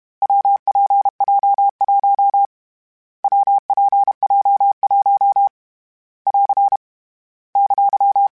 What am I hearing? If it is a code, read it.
Morse code, heard as a keyed tone, about 32 wpm, 800 Hz. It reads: WPJ1 WPJ1 <AR> !
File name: Variable Speed / Variable Speed-mp3s / WPJ WPJ1 sp @30